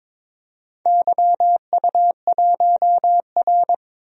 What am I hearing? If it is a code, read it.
Morse code, heard as a keyed tone, about 22 wpm, 700 Hz. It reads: YU1R
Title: DXCC-100 Call Signs - 4 Characters - 22wpm